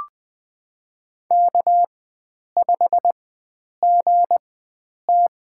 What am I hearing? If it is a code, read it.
Morse code, heard as a keyed tone, about 20 wpm, spaced as for 9 wpm, 700 Hz. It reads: K5GT
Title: USA Call Signs - 4 Characters - 4x Character Spacing - 20wpm